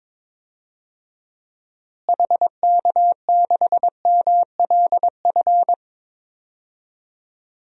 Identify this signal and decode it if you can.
Morse code, heard as a keyed tone, about 22 wpm, 700 Hz. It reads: HK6MLF
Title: DXCC-100 Call Signs - 6 Characters - Repeated 2x - 22wpm